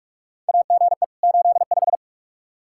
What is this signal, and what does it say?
Morse code, heard as a keyed tone, about 41 wpm, 700 Hz. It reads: AGE 85
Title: Practice QSO - Exchange 3 - More Info - 45wpm